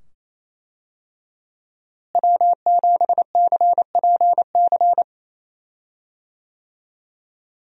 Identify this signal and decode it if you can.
Morse code, heard as a keyed tone, about 27 wpm, 700 Hz. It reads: W7CPC
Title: USA Call Signs - 5 Characters - Repeated 2x - 28wpm